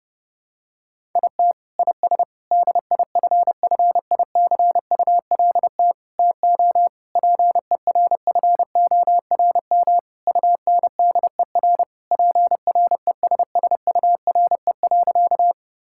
Morse code, heard as a keyed tone, about 32 wpm, 700 Hz. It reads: IT IS DIFFICULT TO PERFORM UNDER PRESSURE.
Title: Sentences from Top 1000 Words - Review Part 6 - 30wpm